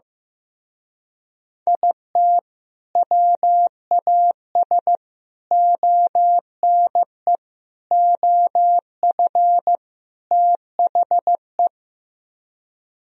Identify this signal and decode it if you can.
Morse code, heard as a keyed tone, about 15 wpm, 700 Hz. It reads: IT WAS ONE OF THE
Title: Speed-Racing - Sets of 5 Words 15wpm